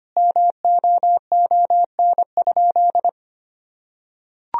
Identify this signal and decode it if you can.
Morse code, heard as a keyed tone, about 25 wpm, 700 Hz. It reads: MOON?
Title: Sentences from Top 1000 Words - Review Part 3 - 25wpm